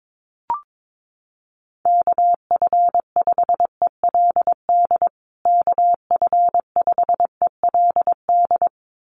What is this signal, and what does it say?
Morse code, heard as a keyed tone, about 22 wpm, 700 Hz. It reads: KF5ELD KF5ELD
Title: USA Call Signs Repeated 2x 22wpm